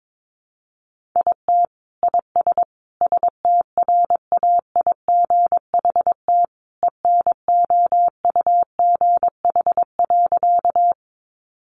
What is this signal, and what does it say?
Morse code, heard as a keyed tone, about 22 wpm, 700 Hz. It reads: IT IS STRAIGHT ENOUGH.
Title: Sentences from Top 1000 Words 22wpm